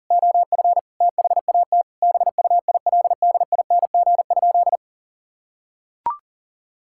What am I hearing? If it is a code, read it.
Morse code, heard as a keyed tone, about 37 wpm, 700 Hz. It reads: OF THAT BUILDING?
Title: Sentences from Top 1000 Words - Review Part 5 - 40wpm